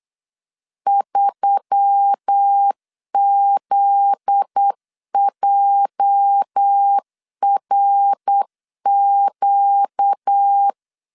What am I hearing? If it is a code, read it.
Morse code, heard as a keyed tone, about 8 wpm, 800 Hz. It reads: 3ZJRQ
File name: LEZIONI / 8 YJ45 / 8 YJ45 8wpm 3214